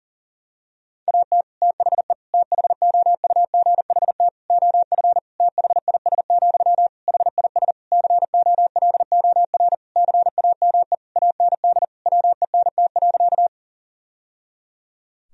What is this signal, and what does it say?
Morse code, heard as a keyed tone, about 40 wpm, 700 Hz. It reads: AT THE THOUGHT OF THIS, HIS COLOR CAME AND WENT.
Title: Sentences from Top 1000 Words - Review Part 2 - 40wpm